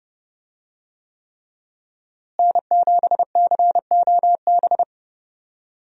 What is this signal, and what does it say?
Morse code, heard as a keyed tone, about 30 wpm, 700 Hz. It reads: N7COB
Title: USA Call Signs - 5 Characters - Repeated 3x - 30wpm